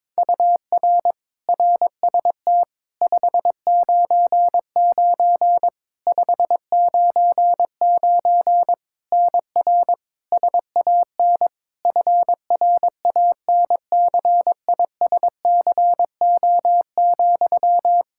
Morse code, heard as a keyed tone, about 22 wpm, 700 Hz. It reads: UR RST 599 599 NR SAN FRANCISCO,
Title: Practice QSO - Exchange 1 - Beginning - 22wpm